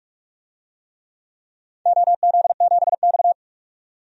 Morse code, heard as a keyed tone, about 41 wpm, 700 Hz. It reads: OZ7X
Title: DXCC-100 Call Signs - 4 Characters - 45wpm